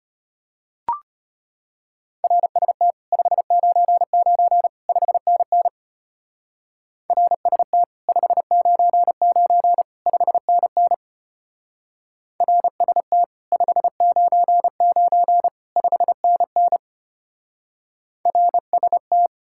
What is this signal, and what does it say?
Morse code, heard as a keyed tone, about 30 wpm, 700 Hz. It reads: RST 599 5NN RST 599 5NN RST 599 5NN RST
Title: Speed-Racing Common QSO Phrases - 25wpm